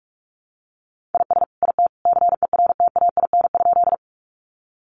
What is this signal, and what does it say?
Morse code, heard as a keyed tone, about 44 wpm, 700 Hz. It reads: IS IT CERTAIN?
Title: Sentences from Top 500 Words 45wpm